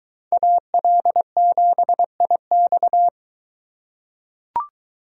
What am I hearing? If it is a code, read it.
Morse code, heard as a keyed tone, about 23 wpm, 700 Hz. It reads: AL7IX